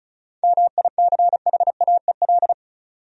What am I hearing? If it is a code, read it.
Morse code, heard as a keyed tone, about 36 wpm, 700 Hz. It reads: MICHAEL
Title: USA Common Names - 35wpm